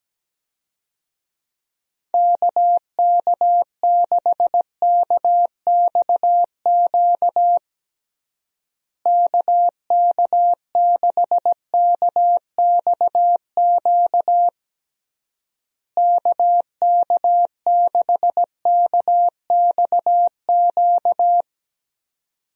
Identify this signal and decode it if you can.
Morse code, heard as a keyed tone, about 17 wpm, 700 Hz. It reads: KK6KXQ KK6KXQ KK6KXQ
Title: USA Call Signs - 6 Characters - Repeated 3x - 17wpm